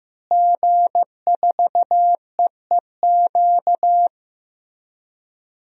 Morse code, heard as a keyed tone, about 15 wpm, 700 Hz. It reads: G4EEQ
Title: DXCC-100 Call Signs - 5 Characters - Repeated 2x - 15wpm